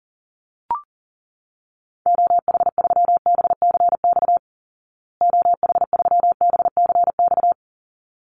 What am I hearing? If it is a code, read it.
Morse code, heard as a keyed tone, about 40 wpm, 700 Hz. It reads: OH3BCX OH3BCX
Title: Call Signs Repeated 3x - 40wpm